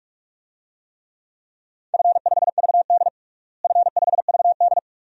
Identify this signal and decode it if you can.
Morse code, heard as a keyed tone, about 46 wpm, 700 Hz. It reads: F5VD F5VD
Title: DXCC-100 Call Signs - 4 Characters - Repeated 2x - 45wpm